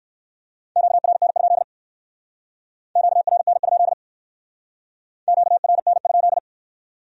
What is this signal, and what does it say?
Morse code, heard as a keyed tone, about 56 wpm, 700 Hz. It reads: QRN? QRN? QRN?
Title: Speed-Racing Q-Codes - 45wpm